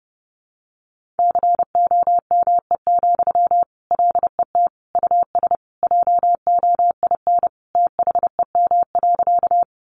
Morse code, heard as a keyed tone, about 30 wpm, 700 Hz. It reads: COME, LET US JOIN THEM.
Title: Sentences from Top 1400 Words 30wpm